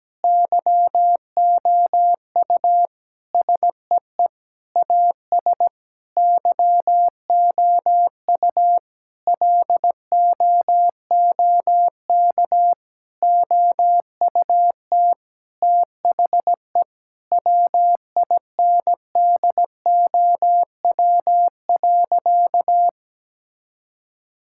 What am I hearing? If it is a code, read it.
Morse code, heard as a keyed tone, about 17 wpm, 700 Hz. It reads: YOU SEE AS YOU LOOK OUT THE WINDOW.
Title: Sentences from Top 1600 Words 17wpm